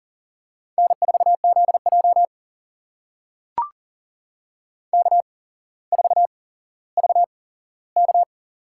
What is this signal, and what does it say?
Morse code, heard as a keyed tone, about 40 wpm, spaced as for 17 wpm, 700 Hz. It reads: N4ZJ K4VK
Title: USA Call Signs - 4 Characters - 8x Character Spacing - 40wpm